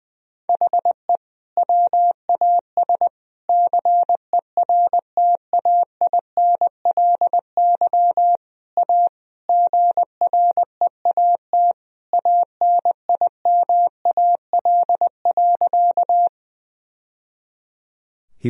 Morse code, heard as a keyed tone, about 20 wpm, 700 Hz. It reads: HE WAS CERTAINLY A GREAT ANIMAL.
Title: Sentences from Top 1000 Words - Review Part 1 - 20wpm